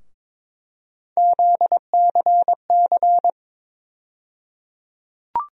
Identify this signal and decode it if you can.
Morse code, heard as a keyed tone, about 22 wpm, 700 Hz. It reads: ZCC
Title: USA Call Sign Postfixes - 3 Letters 22wpm